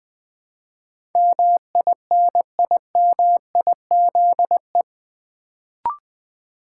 Morse code, heard as a keyed tone, about 20 wpm, 700 Hz. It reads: MINIMIZE
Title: Build A Word - 4 Letters At A Time - 8-Letter Words - 20wpm